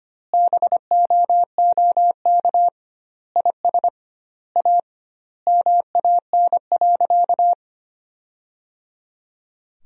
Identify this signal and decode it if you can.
Morse code, heard as a keyed tone, about 25 wpm, 700 Hz. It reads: BOOK IS A MAN.
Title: 2x Word-Spacing - Sentences from Top 500 Words - 25wpm